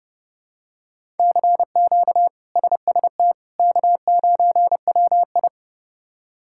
Morse code, heard as a keyed tone, about 32 wpm, 700 Hz. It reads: CQ SST K9WI
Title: K1USN SST QSO Practice - 30wpm